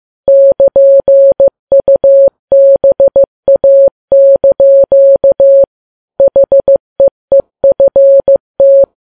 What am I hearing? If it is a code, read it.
Morse code, heard as a keyed tone, about 15 wpm, 550 Hz. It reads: <KN>UBA) HEEFT
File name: Files Web tls / Morse on9fv / morse15kps